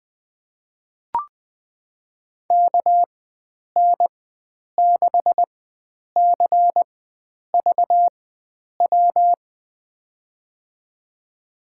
Morse code, heard as a keyed tone, about 21 wpm, spaced as for 9 wpm, 700 Hz. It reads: KN6CVW
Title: USA Call Signs - 6 Characters - 4x Character Spacing - 20wpm